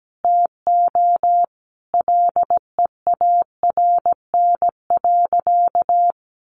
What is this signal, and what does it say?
Morse code, heard as a keyed tone, about 17 wpm, 700 Hz. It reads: TO LEARN.
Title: Sentences from Top 1400 Words 17wpm